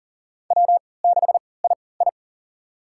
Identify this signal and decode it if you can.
Morse code, heard as a keyed tone, about 40 wpm, spaced as for 23 wpm, 700 Hz. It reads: W6II
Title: USA Call Signs - 4 Characters - 3x Character Spacing - 40wpm